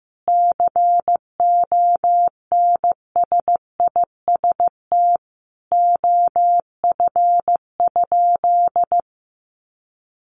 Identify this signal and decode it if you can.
Morse code, heard as a keyed tone, about 14 wpm, 700 Hz. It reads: CONSIST OF?
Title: Sentences from Top 1400 Words 15wpm